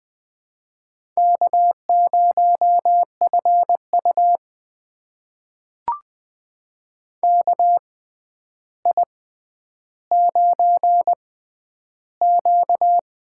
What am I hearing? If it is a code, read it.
Morse code, heard as a keyed tone, about 20 wpm, spaced as for 10 wpm, 700 Hz. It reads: K0FU KI9Q
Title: USA Call Signs - 4 Characters - 6x Character Spacing - 20wpm